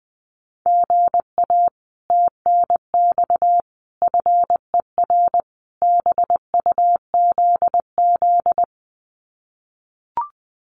Morse code, heard as a keyed tone, about 20 wpm, 700 Hz. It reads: GA TNX FER BUZZ
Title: Common QSO Phrases Encore 20wpm